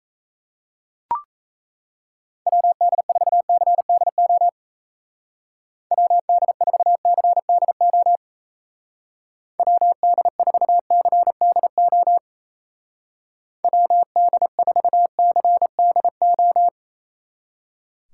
Morse code, heard as a keyed tone, about 35 wpm, 700 Hz. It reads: WD4CDO WD4CDO WD4CDO WD4CDO
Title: Speed Racing - USA 2x3 Call Signs 28wpm